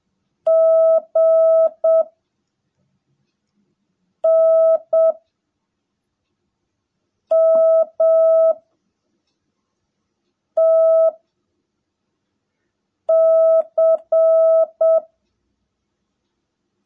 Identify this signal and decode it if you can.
Morse code, heard as a keyed tone, about 7 wpm, 660 Hz. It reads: G N M T C
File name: jh8cbh / 2021cw / cw18a